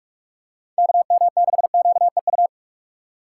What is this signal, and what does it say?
Morse code, heard as a keyed tone, about 46 wpm, 700 Hz. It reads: KM6QEU
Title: USA Call Signs - 2x Character Spacing - 45wpm